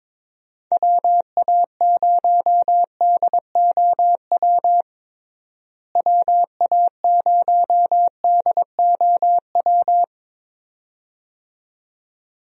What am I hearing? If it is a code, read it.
Morse code, heard as a keyed tone, about 21 wpm, 700 Hz. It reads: WA0DOW WA0DOW
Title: USA Call Signs - 6 Characters - Repeated 2x - 22wpm